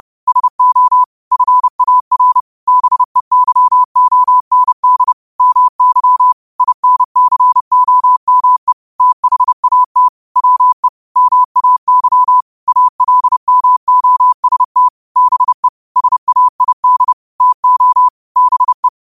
Morse code, heard as a keyed tone, about 30 wpm, 1000 Hz. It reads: SO FAR BEYOND MY INCOME THAT WE MAY ALMOST BE SAID TO BE
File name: qotdfiles / 30wpm